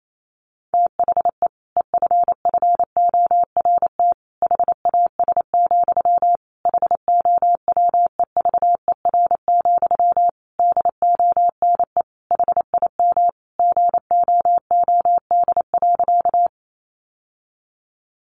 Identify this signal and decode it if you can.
Morse code, heard as a keyed tone, about 28 wpm, 700 Hz. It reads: THE EFFORT HAS, HOWEVER, DONE HIM GOOD.
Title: Sentences from Top 1200 Words 28wpm